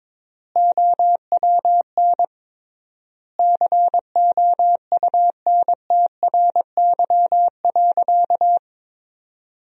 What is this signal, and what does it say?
Morse code, heard as a keyed tone, about 22 wpm, 700 Hz. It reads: OWN COUNTRY.
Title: 3x Word-Spacing - Sentences from Top 300 Words - 22wpm